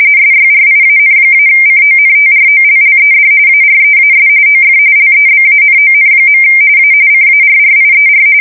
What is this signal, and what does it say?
RTTY, about 45.45 baud. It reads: SYRIAN-CONTROLLED AREAS OF LEBANON. A CARRIER
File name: fsk170 / battleshipnj